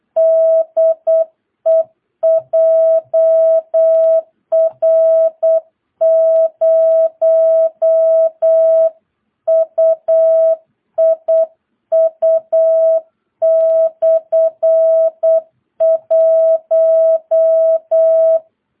Morse code, heard as a keyed tone, about 8 wpm, 650 Hz. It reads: DE JR0UIU/1
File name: jh8cbh / 2022cw